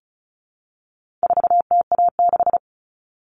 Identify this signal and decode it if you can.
Morse code, heard as a keyed tone, about 34 wpm, 700 Hz. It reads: 4TA6
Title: All Cut Numbers - 4-Digit 35wpm